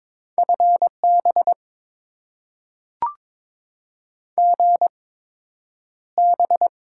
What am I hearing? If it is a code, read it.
Morse code, heard as a keyed tone, about 22 wpm, spaced as for 9 wpm, 700 Hz. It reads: FB GB
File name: Top 90 QSO Elements - 8x Character Spacing - 22wpm